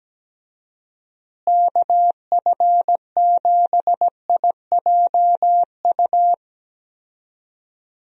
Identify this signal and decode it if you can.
Morse code, heard as a keyed tone, about 17 wpm, 700 Hz. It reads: KF7IJU
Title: USA Call Signs - 6 Characters - Repeated 3x - 17wpm